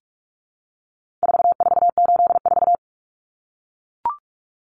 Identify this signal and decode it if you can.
Morse code, heard as a keyed tone, about 44 wpm, 700 Hz. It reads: 4484
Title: Common Cut Numbers - 4-Digit 45wpm